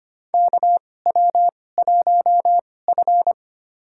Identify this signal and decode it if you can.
Morse code, heard as a keyed tone, about 25 wpm, spaced as for 18 wpm, 700 Hz. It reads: KW1F
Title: USA Call Signs - 4 Characters - 2x Character Spacing - 25wpm